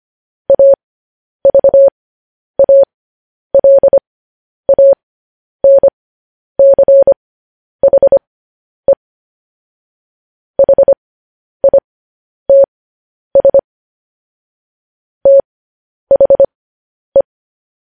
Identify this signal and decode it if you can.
Morse code, heard as a keyed tone, about 25 wpm, spaced as for 10 wpm, 550 Hz. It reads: AVALANCHE HITS THE
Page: News Headlines in Morse Code at 10 WPM